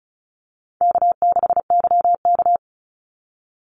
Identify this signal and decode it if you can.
Morse code, heard as a keyed tone, about 36 wpm, 700 Hz. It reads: K6YK
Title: NA SOTA Chaser Callsigns 35wpm